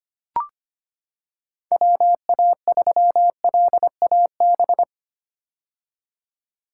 Morse code, heard as a keyed tone, about 25 wpm, 700 Hz. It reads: WA3LAB
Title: K1USN SST QSO Practice - 25wpm